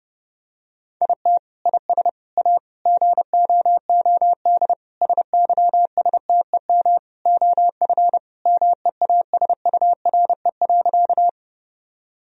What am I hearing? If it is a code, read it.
Morse code, heard as a keyed tone, about 30 wpm, 700 Hz. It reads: IT IS A GOOD SYSTEM OF MEASURE.
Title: Sentences from Top 1000 Words - Review Part 3 - 30wpm